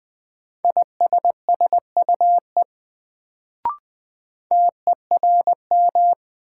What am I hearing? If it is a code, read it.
Morse code, heard as a keyed tone, about 20 wpm, 700 Hz. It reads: ISSUE TERM
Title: Sentences from Top 900 Words - New Word Drill - 20wpm